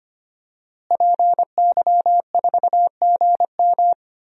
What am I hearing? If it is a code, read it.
Morse code, heard as a keyed tone, about 25 wpm, 700 Hz. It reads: PY4GM
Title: DXCC-100 Call Signs - 5 Characters - Repeated 3x - 25wpm